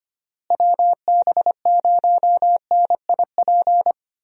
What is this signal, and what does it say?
Morse code, heard as a keyed tone, about 25 wpm, 700 Hz. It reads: WB0NIP
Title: USA Call Signs - 6 Characters - Repeated 2x - 25wpm